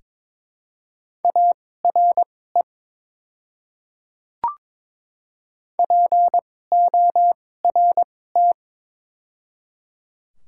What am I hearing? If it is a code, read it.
Morse code, heard as a keyed tone, about 21 wpm, spaced as for 16 wpm, 700 Hz. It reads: ARE PORT
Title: 2x Character Spacing - Top 200 Words - 22wpm